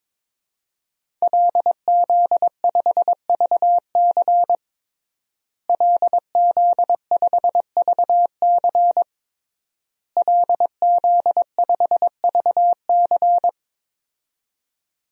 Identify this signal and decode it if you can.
Morse code, heard as a keyed tone, about 22 wpm, 700 Hz. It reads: LZ5VC LZ5VC LZ5VC
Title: DXCC-100 Call Signs - 5 Characters - Repeated 3x - 22wpm